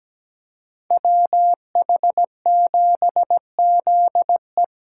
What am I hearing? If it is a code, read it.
Morse code, heard as a keyed tone, about 20 wpm, 700 Hz. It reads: WH7ZE
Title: Speed Racing - USA 2x2 Call Signs 17wpm